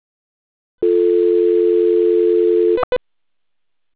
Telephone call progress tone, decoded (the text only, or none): dial tone